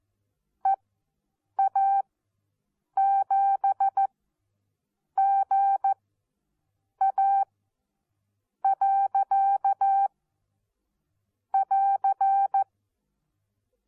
Morse code, heard as a keyed tone, about 14 wpm, spaced as for 7 wpm, 780 Hz. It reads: EA7GA. <AR>
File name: Onlinelehrgang / morsen / DJ4UF Morsen14